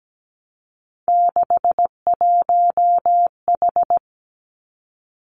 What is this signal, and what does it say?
Morse code, heard as a keyed tone, about 17 wpm, 700 Hz. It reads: 61H
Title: Three Challenging Letter-Number Combinations 17wpm